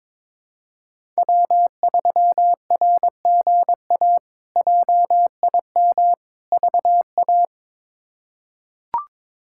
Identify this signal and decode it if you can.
Morse code, heard as a keyed tone, about 21 wpm, 700 Hz. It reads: W3RGA JIM VA
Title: K1USN SST QSO Practice - 22wpm